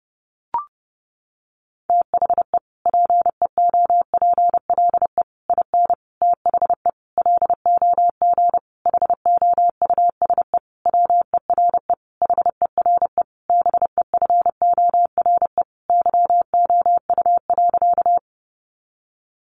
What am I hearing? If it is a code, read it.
Morse code, heard as a keyed tone, about 30 wpm, 700 Hz. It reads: THE PEOPLE IN THE LOG HOUSE WERE HERE BEFORE YOU.
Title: Sentences from Top 1400 Words 30wpm